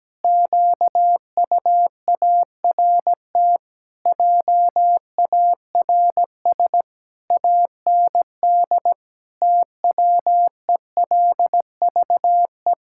Morse code, heard as a keyed tone, about 17 wpm, 700 Hz. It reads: QUART JARS AND TWELVE
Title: Pangram Challenge v2 - 17wpm